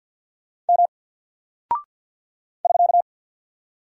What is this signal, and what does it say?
Morse code, heard as a keyed tone, about 48 wpm, 700 Hz. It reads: M <SK>
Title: Single Character (Letter:Number:Punctuation:Prosign) Alt 50wpm